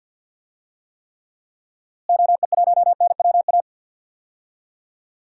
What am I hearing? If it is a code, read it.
Morse code, heard as a keyed tone, about 52 wpm, 700 Hz. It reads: OE1NWA
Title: DXCC-100 Call Signs - 6 Characters - Repeated 2x - 50wpm